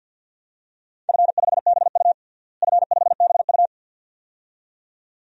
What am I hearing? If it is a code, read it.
Morse code, heard as a keyed tone, about 52 wpm, 700 Hz. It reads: F5BU F5BU
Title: DXCC-100 Call Signs - 4 Characters - Repeated 2x - 50wpm